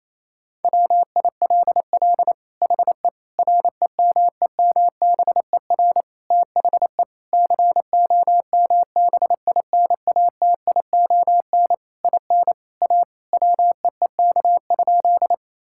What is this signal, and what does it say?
Morse code, heard as a keyed tone, about 27 wpm, 700 Hz. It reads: WILL HE REMEMBER THE COMBINATION IN A WEEK?
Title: Sentences from Top 1000 Words - Review Part 6 - 28wpm